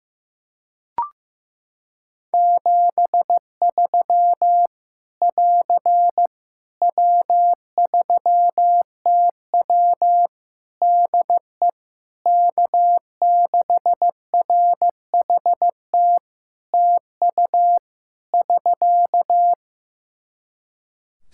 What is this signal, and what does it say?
Morse code, heard as a keyed tone, about 15 wpm, 700 Hz. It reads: 73 <AR> W3TW DE K6RHT TU <SK>
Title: Practice QSO - Exchange 4 - Ending - 15wpm